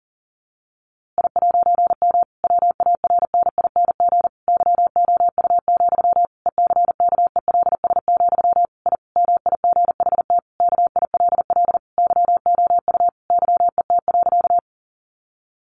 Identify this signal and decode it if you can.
Morse code, heard as a keyed tone, about 37 wpm, 700 Hz. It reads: I'M WARNING YOU, ECKELS, I MIGHT KILL YOU YET.
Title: A Sound Of Thunder 40wpm